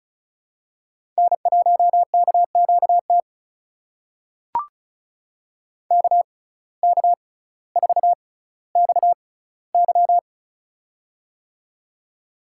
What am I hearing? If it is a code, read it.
Morse code, heard as a keyed tone, about 36 wpm, spaced as for 18 wpm, 700 Hz. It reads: N1KQT KK4XY
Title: USA Call Signs - 5 Characters - 6x Character Spacing - 35wpm